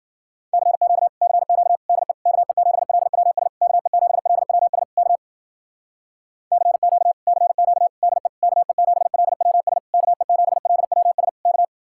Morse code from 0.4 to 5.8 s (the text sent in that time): CQ CQ DE KE7LWS KE7LWS K